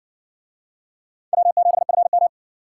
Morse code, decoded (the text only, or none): W7FM